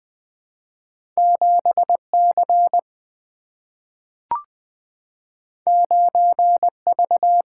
Morse code, heard as a keyed tone, about 20 wpm, 700 Hz. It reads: 7C 9V
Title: Single-Letter Single-Number 20wpm